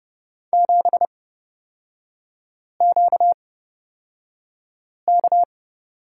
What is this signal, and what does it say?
Morse code, heard as a keyed tone, about 31 wpm, spaced as for 5 wpm, 700 Hz. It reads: 7QK
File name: Lesson 38e - Letter q - Three Characters 30wpm